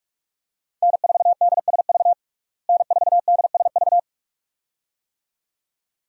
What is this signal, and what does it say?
Morse code, heard as a keyed tone, about 46 wpm, 700 Hz. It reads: N4DSV N4DSV
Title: USA Call Signs - 5 Characters - Repeated 2x - 45wpm